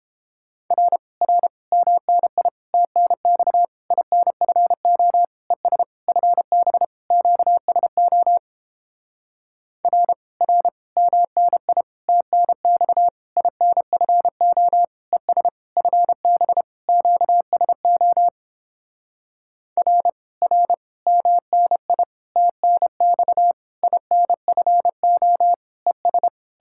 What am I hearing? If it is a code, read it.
Morse code, heard as a keyed tone, about 29 wpm, 700 Hz. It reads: R R MNI TNX INFO ES FB QSO R R MNI TNX INFO ES FB QSO R R MNI TNX INFO ES